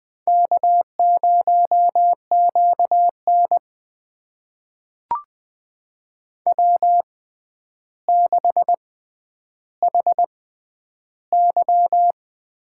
Morse code, heard as a keyed tone, about 20 wpm, spaced as for 10 wpm, 700 Hz. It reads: K0QN W6HY
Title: USA Call Signs - 4 Characters - 6x Character Spacing - 20wpm